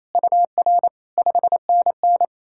Morse code, heard as a keyed tone, about 38 wpm, 700 Hz. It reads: UR 5NN